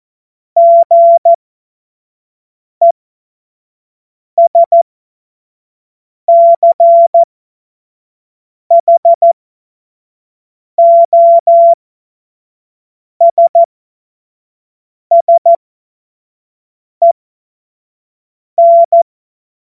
Morse code, heard as a keyed tone, about 14 wpm, spaced as for 5 wpm, 680 Hz. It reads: GESCHOSSEN
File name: Morsekurs / Witze / Witz 13. April 2021, 25 F70